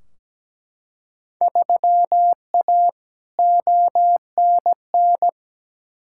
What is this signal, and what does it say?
Morse code, heard as a keyed tone, about 17 wpm, 700 Hz. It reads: 3A ONN
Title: Field Day Exchanges Encore - 17wpm